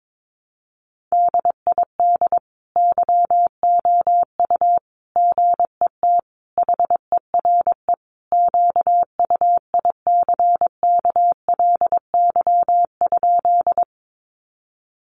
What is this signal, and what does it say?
Morse code, heard as a keyed tone, about 22 wpm, 700 Hz. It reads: DID YOU GET HERE QUICKLY?
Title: Sentences from Top 700 Words 22wpm